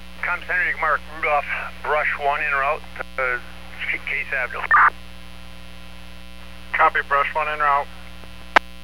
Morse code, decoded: none

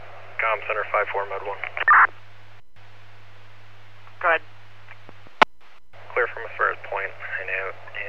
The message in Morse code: none

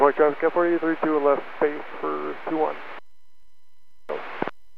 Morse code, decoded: none

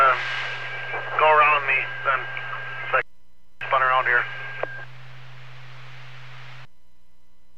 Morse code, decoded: none

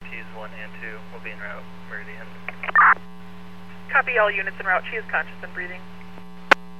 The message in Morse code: none

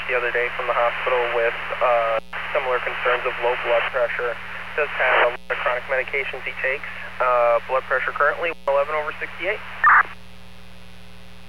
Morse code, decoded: none

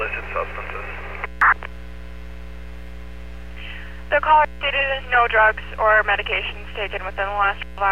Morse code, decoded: none

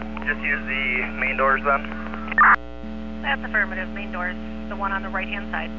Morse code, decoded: none

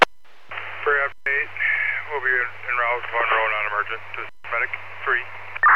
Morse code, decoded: none

